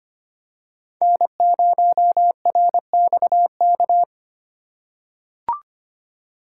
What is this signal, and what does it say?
Morse code, heard as a keyed tone, about 25 wpm, 700 Hz. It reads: N0RXK
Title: USA Call Signs - 6x Character Spacing - 25wpm